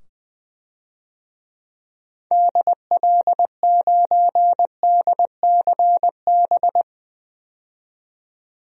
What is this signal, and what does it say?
Morse code, heard as a keyed tone, about 20 wpm, 700 Hz. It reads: DL9DCB
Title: DXCC-100 Call Signs - 6 Characters - 20wpm